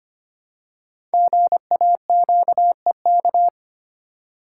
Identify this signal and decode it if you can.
Morse code, heard as a keyed tone, about 25 wpm, 700 Hz. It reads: GAQEK
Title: Code Groups - 5 Letters - 25wpm